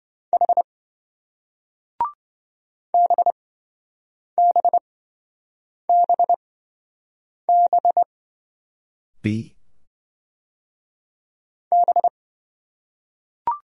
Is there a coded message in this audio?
Yes: Morse code from 0.3 to 12.1 s, heard as a keyed tone, about 27 wpm, spaced as for 7 wpm, 700 Hz. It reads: H BBBB B